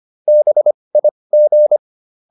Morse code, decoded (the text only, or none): BIG